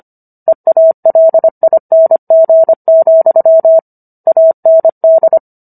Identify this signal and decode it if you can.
Morse code, heard as a keyed tone, about 25 wpm, 650 Hz. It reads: EALING, AND